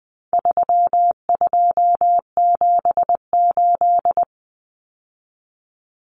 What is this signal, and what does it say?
Morse code, heard as a keyed tone, about 20 wpm, 700 Hz. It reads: 3278
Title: Common Cut Numbers - 4-Digit 20wpm